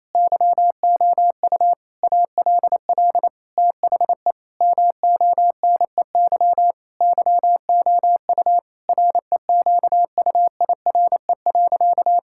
Morse code, heard as a keyed tone, about 28 wpm, 700 Hz. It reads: YOU ALL THE MONEY YOU REQUIRE.
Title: Sentences from Top 1000 Words - Review Part 6 - 28wpm